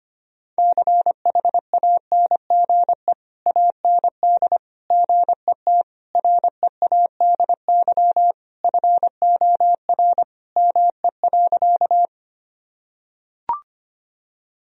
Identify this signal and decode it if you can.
Morse code, heard as a keyed tone, about 25 wpm, 700 Hz. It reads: CHANGE AND GET READY FOR ME.
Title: Sentences from Top 1000 Words - Review Part 1 - 25wpm